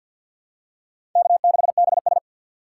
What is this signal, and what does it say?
Morse code, heard as a keyed tone, about 51 wpm, 700 Hz. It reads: K6BS
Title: USA Call Signs - 4 Characters - 2x Character Spacing - 50wpm